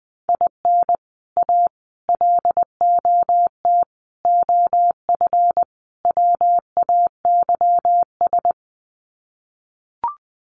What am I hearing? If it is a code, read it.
Morse code, heard as a keyed tone, about 20 wpm, 700 Hz. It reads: IN A LOT OF WAYS
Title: Sets of 5 Words Encore 20wpm